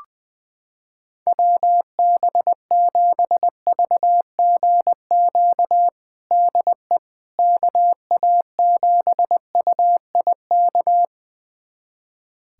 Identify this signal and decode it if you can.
Morse code, heard as a keyed tone, about 20 wpm, 700 Hz. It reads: WB7VGQ DE KA7UIK
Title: USA Callsign DE Callsign - 6 Characters - 20wpm